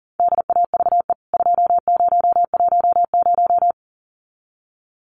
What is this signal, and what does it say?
Morse code, heard as a keyed tone, about 40 wpm, 700 Hz. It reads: DAVE 2010
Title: CWT Exchange 40wpm